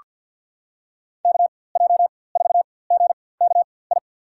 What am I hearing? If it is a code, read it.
Morse code, heard as a keyed tone, about 47 wpm, spaced as for 23 wpm, 700 Hz. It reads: KJ4GKI
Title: USA Call Signs - 6 Characters - 4x Character Spacing - 50wpm